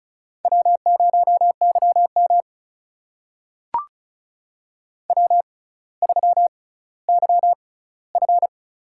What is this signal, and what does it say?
Morse code, heard as a keyed tone, about 36 wpm, spaced as for 18 wpm, 700 Hz. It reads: W0YM W3YF
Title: USA Call Signs - 4 Characters - 6x Character Spacing - 35wpm